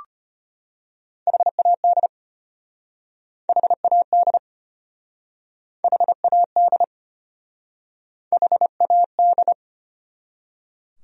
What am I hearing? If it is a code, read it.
Morse code, heard as a keyed tone, about 33 wpm, 700 Hz. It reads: HAD HAD HAD HAD